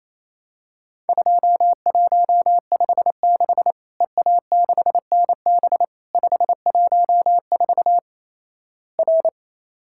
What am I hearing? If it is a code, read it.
Morse code, heard as a keyed tone, about 27 wpm, 700 Hz. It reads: 2156 EA6NB 514 R
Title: Worked All Europe DX Contest - QTC Exchange - 28wpm